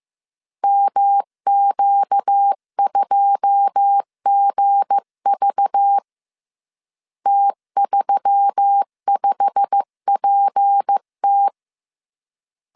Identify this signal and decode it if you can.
Morse code, heard as a keyed tone, about 15 wpm, 800 Hz. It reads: MQ2GV T35PT
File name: CW / DET / 1 9 1500C S B 14wpm 3302